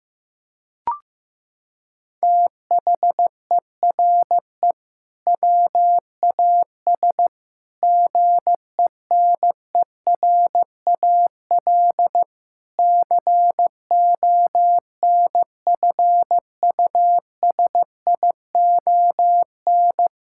Morse code, heard as a keyed tone, about 15 wpm, 700 Hz. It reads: THERE WAS GENERAL CONFUSION
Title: Sentences from Top 1600 Words 15wpm